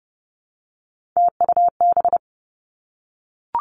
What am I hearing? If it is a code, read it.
Morse code, heard as a keyed tone, about 30 wpm, 700 Hz. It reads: TUB
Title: Three Challenging Letter-Number Combinations 30wpm